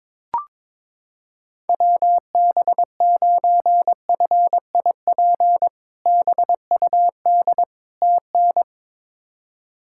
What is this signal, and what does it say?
Morse code, heard as a keyed tone, about 22 wpm, 700 Hz. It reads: WB9FIP BUD TN
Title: K1USN SST QSO Practice - 22wpm